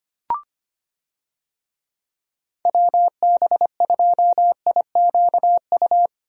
Morse code, heard as a keyed tone, about 25 wpm, 700 Hz. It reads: WB2IQU
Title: USA Call Signs - 6 Characters - Repeated 3x - 25wpm